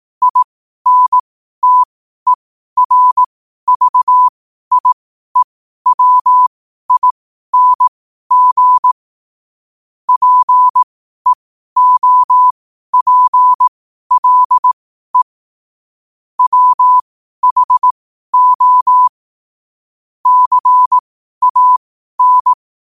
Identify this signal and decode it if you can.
Morse code, heard as a keyed tone, about 18 wpm, spaced as for 12 wpm, 1000 Hz. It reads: INTERVIEWING PEOPLE WHO CAN
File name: qotdfiles / 12wpm